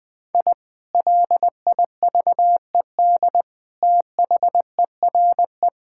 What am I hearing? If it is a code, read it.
Morse code, heard as a keyed tone, about 20 wpm, 700 Hz. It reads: I LIVED THERE
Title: Sentences with 3-Word Phrases - 20wpm